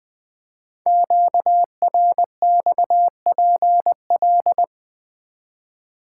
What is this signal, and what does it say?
Morse code, heard as a keyed tone, about 20 wpm, 700 Hz. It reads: QRXPL
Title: Code Groups - 5 Letters - 20wpm